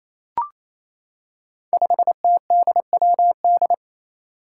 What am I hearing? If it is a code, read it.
Morse code, heard as a keyed tone, about 28 wpm, 700 Hz. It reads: 5TDWD
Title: Five Challenging Letter-Number Combinations 28wpm